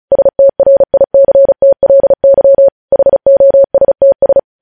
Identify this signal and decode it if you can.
Morse code, heard as a keyed tone, about 36 wpm, 550 Hz. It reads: STRICTLY HOSTS
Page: news headlines in morse code at 35 wpm